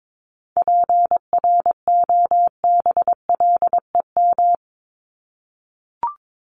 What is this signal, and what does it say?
Morse code, heard as a keyed tone, about 22 wpm, 700 Hz. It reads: PROBLEM
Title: Top 500 Words Repeated 22wpm